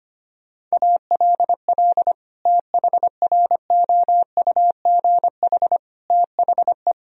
Morse code, heard as a keyed tone, about 25 wpm, 700 Hz. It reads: ALL THROUGH THE
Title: Sentences from Top 1000 Words - Review Part 1 - 25wpm